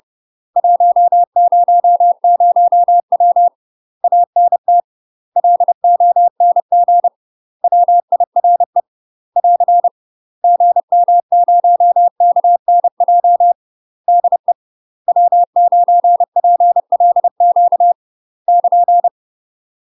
Morse code, heard as a keyed tone, about 30 wpm, 700 Hz. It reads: 100W ANT LONG WIRE <AR> GM0KNJ DE W9PLQ <KN>
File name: UK MASTER QSOs TRAINING / 30WPM / UK MASTER QSOs 0016